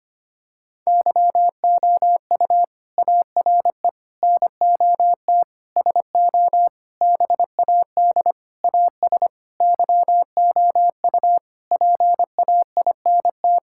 Morse code, heard as a keyed tone, about 25 wpm, 700 Hz. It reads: YOU ARE NOT SO BAD AS YOU PAINT
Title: Sentences from Top 1000 Words - Review Part 6 - 25wpm